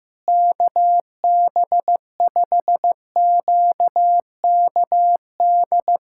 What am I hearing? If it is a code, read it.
Morse code, heard as a keyed tone, about 16 wpm, 700 Hz. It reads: KB5QKD
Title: USA Callsign DE Callsign - 6 Characters - 15wpm